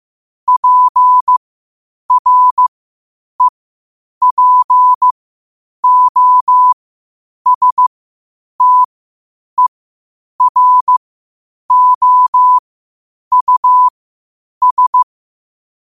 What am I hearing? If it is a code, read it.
Morse code, heard as a keyed tone, about 15 wpm, spaced as for 8 wpm, 1000 Hz. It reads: PREPOSTEROUS
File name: qotdfiles / 8wpm